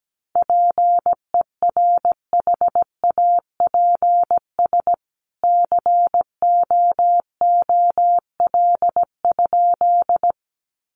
Morse code, heard as a keyed tone, about 17 wpm, 700 Hz. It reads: PERHAPS COOL?
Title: Sentences from Top 1200 Words 17wpm